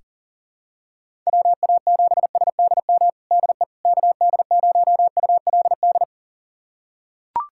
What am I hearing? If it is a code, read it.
Morse code, heard as a keyed tone, about 40 wpm, 700 Hz. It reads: WA7SDM DE KD0ULD
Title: USA Callsign DE Callsign - 6 Characters - 40wpm